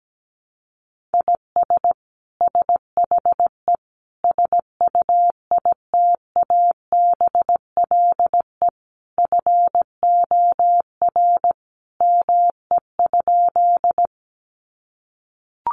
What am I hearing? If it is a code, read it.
Morse code, heard as a keyed tone, about 17 wpm, 700 Hz. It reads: IS SHE SUITABLE FOR ME?
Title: Sentences from Top 1400 Words 17wpm